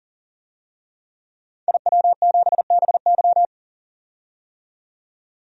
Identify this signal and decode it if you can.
Morse code, heard as a keyed tone, about 40 wpm, 700 Hz. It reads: IW7BY
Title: DXCC-100 Call Signs - 5 Characters - 40wpm